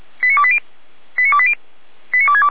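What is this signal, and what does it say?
Five-tone selective call (ZVEI), heard as quick runs of five tones, tones of about 70 ms. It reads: 89280 89280 89262
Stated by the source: ZVEI five-tone selective call 89315 received on an SDR, from the Feuerwehr Orlamünde audio archive.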